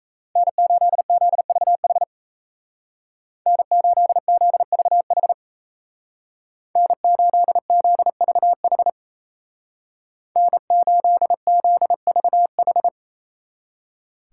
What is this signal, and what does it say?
Morse code, heard as a keyed tone, about 36 wpm, 700 Hz. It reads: N8ZVH N8ZVH N8ZVH N8ZVH